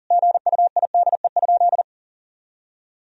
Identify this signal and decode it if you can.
Morse code, heard as a keyed tone, about 40 wpm, 700 Hz. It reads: GUIDE?
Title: Sentences from Top 1000 Words - Review Part 3 - 40wpm